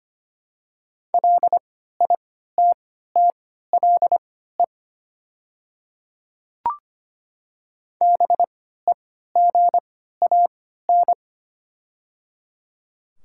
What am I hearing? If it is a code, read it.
Morse code, heard as a keyed tone, about 25 wpm, spaced as for 14 wpm, 700 Hz. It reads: LITTLE BEGAN
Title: 3x Character Spacing - Top 500 Words - 25wpm